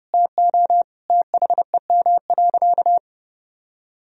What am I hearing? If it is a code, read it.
Morse code, heard as a keyed tone, about 30 wpm, 700 Hz. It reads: TO THEM.
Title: Sentences from Top 1000 Words - Review Part 2 - 30wpm